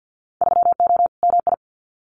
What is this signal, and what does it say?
Morse code, heard as a keyed tone, about 50 wpm, 700 Hz. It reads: 3O MI
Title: Winter Field Day Exchanges Encore 50wpm